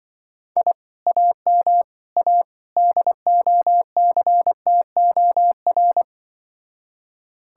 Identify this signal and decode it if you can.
Morse code, heard as a keyed tone, about 23 wpm, 700 Hz. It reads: I AM A DOCTOR
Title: Speed-Racing Common QSO Phrases - 20wpm